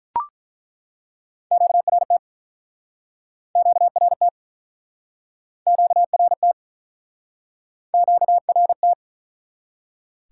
Morse code, heard as a keyed tone, about 48 wpm, 700 Hz. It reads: QRT QRT QRT QRT